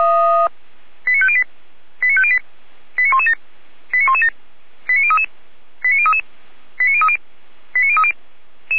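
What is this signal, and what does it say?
Five-tone selective call (ZVEI), heard as quick runs of five tones, tones of about 70 ms. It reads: 89508 89508 89117 89117 89033 89033 89030 89030
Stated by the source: ZVEI five-tone selective call received on an SDR.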